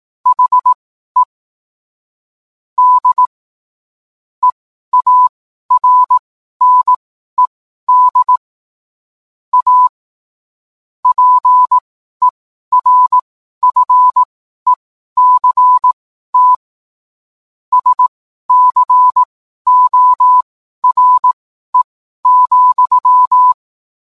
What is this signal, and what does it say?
Morse code, heard as a keyed tone, about 18 wpm, spaced as for 12 wpm, 1000 Hz. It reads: HE D EARNED A PERFECT SCORE,